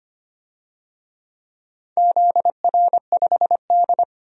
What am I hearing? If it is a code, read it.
Morse code, heard as a keyed tone, about 25 wpm, 700 Hz. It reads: ZR5D
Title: DXCC-100 Call Signs - 4 Characters - 25wpm